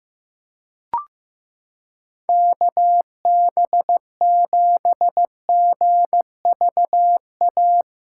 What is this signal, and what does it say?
Morse code, heard as a keyed tone, about 15 wpm, 700 Hz. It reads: KB7GVA
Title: USA Callsign DE Callsign - 6 Characters - 15wpm